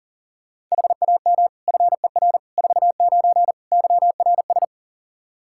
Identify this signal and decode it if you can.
Morse code, heard as a keyed tone, about 40 wpm, 700 Hz. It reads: HAM FER 49 YRS
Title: Practice QSO - Exchange 3 - More Info - 40wpm